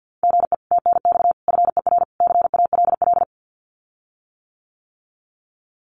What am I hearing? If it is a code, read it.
Morse code, heard as a keyed tone, about 50 wpm, 700 Hz. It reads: GE TNX FER CALL
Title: Common QSO Phrases Encore 50wpm